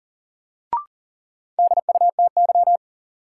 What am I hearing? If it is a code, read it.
Morse code, heard as a keyed tone, about 40 wpm, 700 Hz. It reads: DUTY
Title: Sentences from Top 1500 Words - New Word Drill Alt - 40wpm